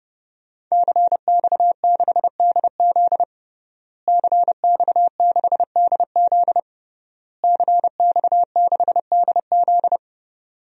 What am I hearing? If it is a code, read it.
Morse code, heard as a keyed tone, about 30 wpm, 700 Hz. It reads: CX6DZ CX6DZ CX6DZ
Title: DXCC-100 Call Signs - 5 Characters - Repeated 3x - 30wpm